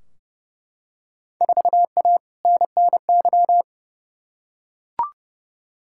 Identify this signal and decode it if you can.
Morse code, heard as a keyed tone, about 30 wpm, 700 Hz. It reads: 4A NNY
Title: Field Day Exchanges - 30wpm